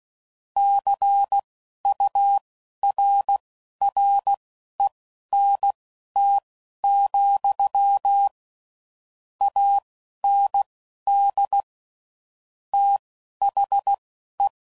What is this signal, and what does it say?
Morse code, heard as a keyed tone, about 16 wpm, spaced as for 11 wpm, 800 Hz. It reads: CURRENT, AND THE
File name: CW Code Groups @10WPM / Technical Subjects / XL